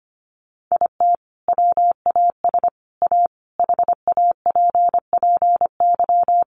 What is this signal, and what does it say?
Morse code, heard as a keyed tone, about 25 wpm, 700 Hz. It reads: IT WAS A HAPPY
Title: Sentences from Top 1000 Words 25wpm